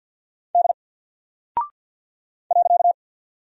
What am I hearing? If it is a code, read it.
Morse code, heard as a keyed tone, about 50 wpm, 700 Hz. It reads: D .
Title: Single Character (Letter:Number:Punctuation:Prosign) 50wpm